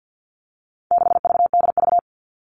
Morse code, heard as a keyed tone, about 50 wpm, 700 Hz. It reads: 6VDV
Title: All Cut Numbers - 4-Digit 50wpm